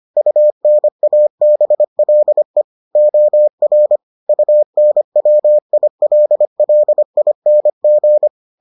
Morse code, heard as a keyed tone, about 25 wpm, 600 Hz. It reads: UNABLE OR UNWILLING